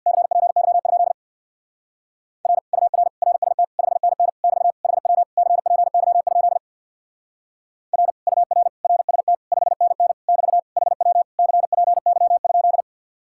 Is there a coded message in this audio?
Yes: Morse code from 0.1 to 12.8 s, heard as a keyed tone, about 65 wpm, 700 Hz. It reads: CPY? R UR RST 5NN <BT> HW CPY? R UR RST 5NN <BT> HW CPY?